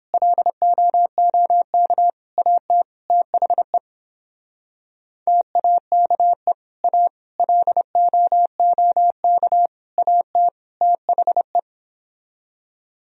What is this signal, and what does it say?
Morse code, heard as a keyed tone, about 29 wpm, 700 Hz. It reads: LOOK AT THE TAKE A LOOK AT THE